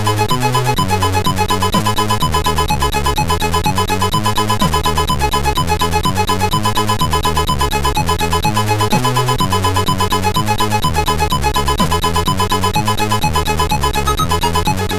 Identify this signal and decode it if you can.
There is no code.